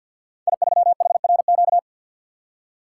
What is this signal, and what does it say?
Morse code, heard as a keyed tone, about 50 wpm, 700 Hz. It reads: I3HRQ
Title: Code Groups - 5 Letter:Numbers - 50wpm